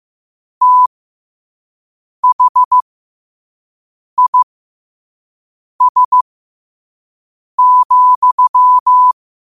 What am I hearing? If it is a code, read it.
Morse code, heard as a keyed tone, about 15 wpm, spaced as for 5 wpm, 1000 Hz. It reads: THIS,
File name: qotdfiles / 5wpm